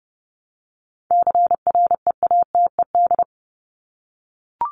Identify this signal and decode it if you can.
Morse code, heard as a keyed tone, about 30 wpm, 700 Hz. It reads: CREATED
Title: ED Suffix - Top 100 Words with Base Words 30wpm